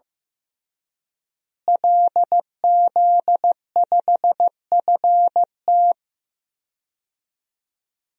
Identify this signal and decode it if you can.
Morse code, heard as a keyed tone, about 15 wpm, 700 Hz. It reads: LZ5FT
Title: DXCC-100 Call Signs - 5 Characters - Repeated 2x - 15wpm